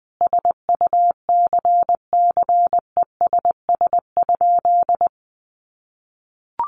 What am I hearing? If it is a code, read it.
Morse code, heard as a keyed tone, about 20 wpm, 700 Hz. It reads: SUCCESS?
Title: Sentences from Top 900 Words 20wpm